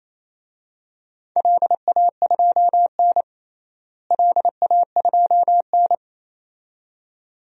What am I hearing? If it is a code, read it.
Morse code, heard as a keyed tone, about 28 wpm, 700 Hz. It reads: LA2N LA2N
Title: DXCC-100 Call Signs - 4 Characters - Repeated 2x - 28wpm